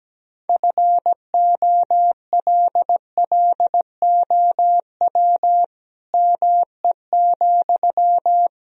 Morse code, heard as a keyed tone, about 17 wpm, 700 Hz. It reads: FOLLOW ME,
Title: Sentences from Top 200 Words and Abbreviations 17wpm